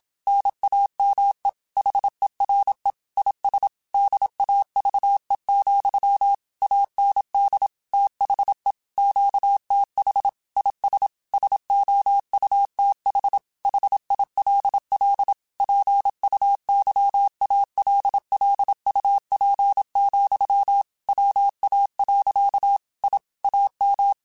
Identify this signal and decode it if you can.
Morse code, heard as a keyed tone, about 27 wpm, 800 Hz. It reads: NAME HERE IS DAVE, AND THE QTH IS SOUTH HILL PUYALLUP, WA. I AM
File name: QSOs / QSOs-mp3s / WA7QOX 2@25